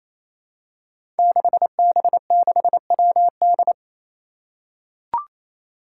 Morse code, heard as a keyed tone, about 28 wpm, 700 Hz. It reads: 6B6WD
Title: Five Challenging Letter-Number Combinations 28wpm